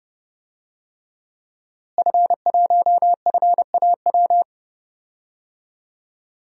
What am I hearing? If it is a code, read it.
Morse code, heard as a keyed tone, about 30 wpm, 700 Hz. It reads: F1FAW
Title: DXCC-100 Call Signs - 5 Characters - Repeated 3x - 30wpm